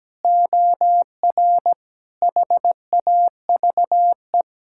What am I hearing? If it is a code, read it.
Morse code, heard as a keyed tone, about 17 wpm, 700 Hz. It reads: OR HAVE
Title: Sentences from Top 1000 Words - Review Part 6 - 17wpm